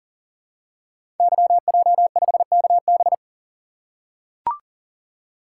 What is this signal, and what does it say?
Morse code, heard as a keyed tone, about 39 wpm, 700 Hz. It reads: YJ5KB
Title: Code Groups - 5 Letter:Numbers - 40wpm